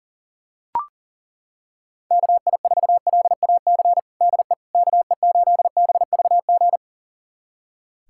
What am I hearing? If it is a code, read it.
Morse code, heard as a keyed tone, about 40 wpm, 700 Hz. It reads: KI4LAC DE KE8BVG
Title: USA Callsign DE Callsign - 6 Characters - 40wpm